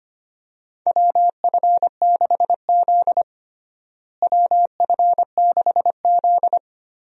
Morse code, heard as a keyed tone, about 25 wpm, 700 Hz. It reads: WF6Z WF6Z
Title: USA Call Signs - 4 Characters - Repeated 3x - 25wpm